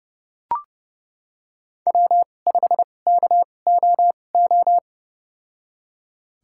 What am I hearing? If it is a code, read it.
Morse code, heard as a keyed tone, about 30 wpm, spaced as for 22 wpm, 700 Hz. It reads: W5KOO
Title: USA Call Signs - 5 Characters - 2x Character Spacing - 30wpm